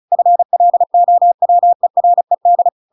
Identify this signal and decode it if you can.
Morse code, heard as a keyed tone, about 34 wpm, 700 Hz. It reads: FLOWERED
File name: Pinocchio / Pinocchio ch8 35 35 ik1hge